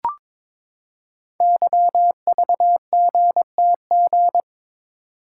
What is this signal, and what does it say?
Morse code, heard as a keyed tone, about 22 wpm, 700 Hz. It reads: YVGTG
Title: Code Groups - 5 Letters - 22wpm